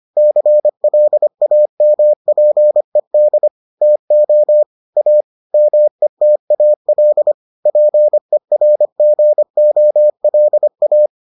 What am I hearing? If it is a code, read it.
Morse code, heard as a keyed tone, about 25 wpm, 600 Hz. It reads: CLAMPED TO A METAL PERGOLA